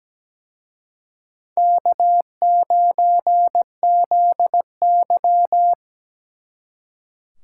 Morse code, heard as a keyed tone, about 20 wpm, 700 Hz. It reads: K9ZY
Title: Speed Racing - USA 1x2 Call Signs 17wpm